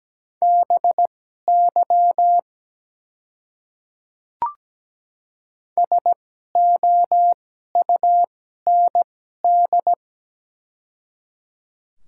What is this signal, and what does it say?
Morse code, heard as a keyed tone, about 17 wpm, spaced as for 12 wpm, 700 Hz. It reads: BY SOUND
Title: 2x Character Spacing - Top 300 Words - 17wpm